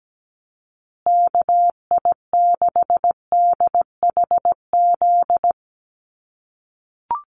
Morse code, heard as a keyed tone, about 17 wpm, 700 Hz. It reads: KI6DHZ
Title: USA Call Signs Repeated 2x 17wpm